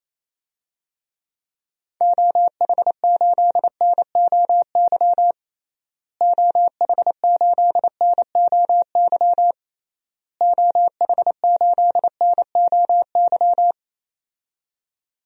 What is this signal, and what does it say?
Morse code, heard as a keyed tone, about 27 wpm, 700 Hz. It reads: OH8NOY OH8NOY OH8NOY
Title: DXCC-100 Call Signs - 6 Characters - Repeated 3x - 28wpm